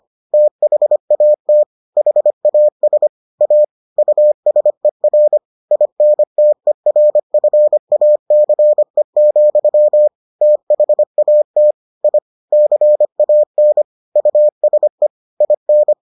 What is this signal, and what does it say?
Morse code, heard as a keyed tone, about 25 wpm, 600 Hz. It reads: THAT HAS A USER INTERFACE, THAT I CAN USE IN